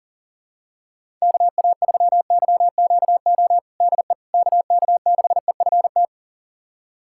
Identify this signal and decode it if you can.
Morse code, heard as a keyed tone, about 40 wpm, 700 Hz. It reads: KA3YQO DE KK6EFT
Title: USA Callsign DE Callsign - 6 Characters - 40wpm